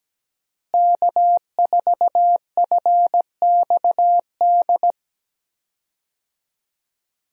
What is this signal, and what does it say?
Morse code, heard as a keyed tone, about 17 wpm, 700 Hz. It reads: K4FXD
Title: USA Call Signs - 5 Characters - Repeated 3x - 17wpm